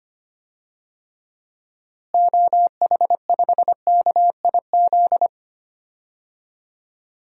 Morse code, heard as a keyed tone, about 25 wpm, 700 Hz. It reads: OH5KIZ
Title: DXCC-100 Call Signs - 6 Characters - Repeated 3x - 25wpm